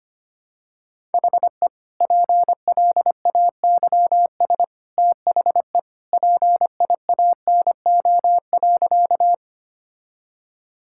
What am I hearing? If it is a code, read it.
Morse code, heard as a keyed tone, about 25 wpm, 700 Hz. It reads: HE PLAYS THE PIANO.
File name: Easy Sentences with Hints - 25wpm